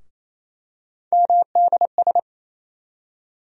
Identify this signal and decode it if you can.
Morse code, heard as a keyed tone, about 28 wpm, 700 Hz. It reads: MDS
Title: USA Call Sign Postfixes - 3 Letters 28wpm